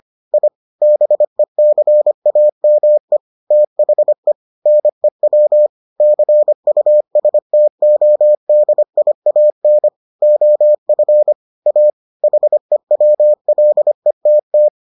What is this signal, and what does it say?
Morse code, heard as a keyed tone, about 25 wpm, 600 Hz. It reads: I BECAME THE NEW CUSTODIAN OF A HEWLETT